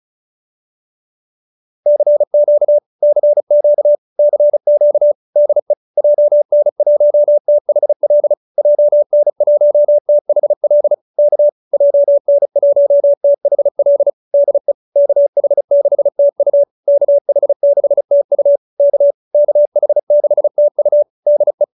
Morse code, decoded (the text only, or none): CQ CQ CQ DE JN1THL JN1THL K JN1THL DE KH6TU KH6TU K KH6TU DE